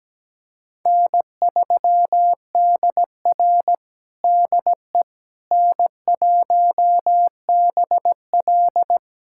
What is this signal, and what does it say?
Morse code, heard as a keyed tone, about 17 wpm, 700 Hz. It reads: N3DR DE N1BL
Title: USA Callsign DE Callsign - 4 Characters - 17wpm